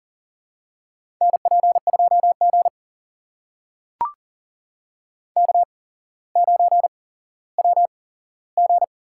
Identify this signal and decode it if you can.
Morse code, heard as a keyed tone, about 40 wpm, spaced as for 17 wpm, 700 Hz. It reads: NP2G K9WG
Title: USA Call Signs - 4 Characters - 8x Character Spacing - 40wpm